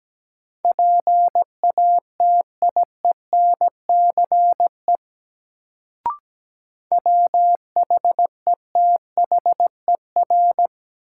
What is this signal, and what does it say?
Morse code, heard as a keyed tone, about 17 wpm, 700 Hz. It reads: PATIENCE WHETHER
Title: Sentences from Top 1300 Words - New Word Drill Alt - 17wpm